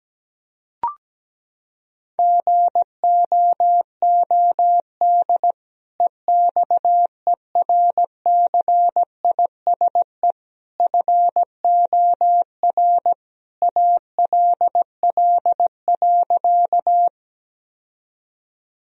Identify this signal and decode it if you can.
Morse code, heard as a keyed tone, about 17 wpm, 700 Hz. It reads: GOOD EXERCISE FOR ALL.
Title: Sentences from Top 1000 Words - Review Part 4 - 17wpm